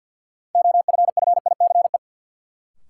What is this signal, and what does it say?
Morse code, heard as a keyed tone, about 50 wpm, 700 Hz. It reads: OFFICE
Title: Sentences from Top 1000 Words - New Word Drill - 50wpm